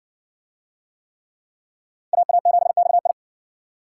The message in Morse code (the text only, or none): AA7XI